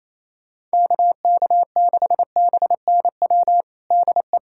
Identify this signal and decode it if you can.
Morse code, heard as a keyed tone, about 28 wpm, 700 Hz. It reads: KK6BNW DE
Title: USA Callsign DE Callsign 28wpm